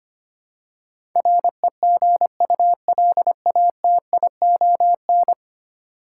Morse code, heard as a keyed tone, about 25 wpm, 700 Hz. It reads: REGULATION
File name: Ten Letter Words - 25wpm